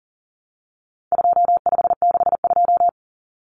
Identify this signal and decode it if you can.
Morse code, heard as a keyed tone, about 40 wpm, 700 Hz. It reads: 2562
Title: Common Cut Numbers - 4-Digit 40wpm